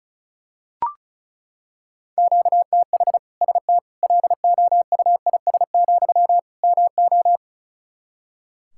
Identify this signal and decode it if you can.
Morse code, heard as a keyed tone, about 36 wpm, 700 Hz. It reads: QTH ST LOUIS, MO
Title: Practice QSO - Exchange 1 - Beginning - 35wpm